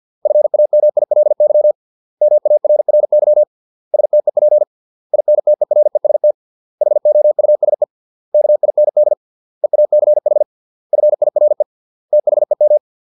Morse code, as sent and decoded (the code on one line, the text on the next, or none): ..-. .- -- .. .-.. -.-- / -- .- .-. .-. -.-- / ... - . .--. / .. -. - . .-. ... - / .... --- ..- ... / -.- .. -. -.. / . .- -.-. .... / ..-. .. .-. / - .... --
FAMILY MARRY STEP INTEREST HOUSE KIND EACH FIRE THEM